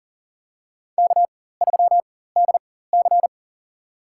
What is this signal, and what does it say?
Morse code, heard as a keyed tone, about 37 wpm, spaced as for 18 wpm, 700 Hz. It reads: K3DC
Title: USA Call Signs - 4 Characters - 4x Character Spacing - 40wpm